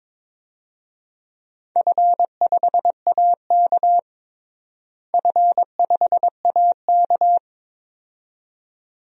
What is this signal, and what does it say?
Morse code, heard as a keyed tone, about 22 wpm, 700 Hz. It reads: F5AK F5AK
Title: DXCC-100 Call Signs - 4 Characters - Repeated 2x - 22wpm